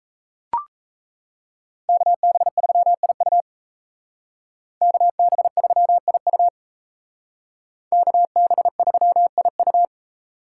Morse code, heard as a keyed tone, about 36 wpm, 700 Hz. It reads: KB3IU KB3IU KB3IU